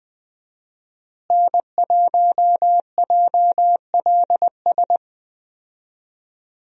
Morse code, heard as a keyed tone, about 20 wpm, 700 Hz. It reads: N1JLS
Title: USA Call Signs - 5 Characters - Repeated 2x - 20wpm